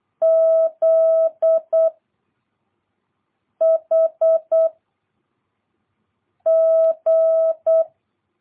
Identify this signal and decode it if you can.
Morse code, heard as a keyed tone, about 8 wpm, 660 Hz. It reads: Z H G